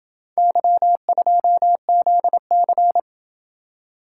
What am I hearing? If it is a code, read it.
Morse code, heard as a keyed tone, about 27 wpm, 700 Hz. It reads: Y2ZC